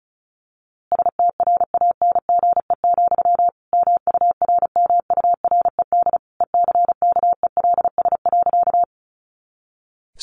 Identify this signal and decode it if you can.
Morse code, heard as a keyed tone, about 34 wpm, 700 Hz. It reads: STRANGE, MURMURED ECKELS.
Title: A Sound Of Thunder 35wpm